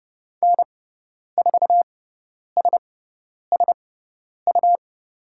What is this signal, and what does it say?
Morse code, heard as a keyed tone, about 30 wpm, spaced as for 10 wpm, 700 Hz. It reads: N4SSU
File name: Lesson 17g - Letter u - Call Signs 30wpm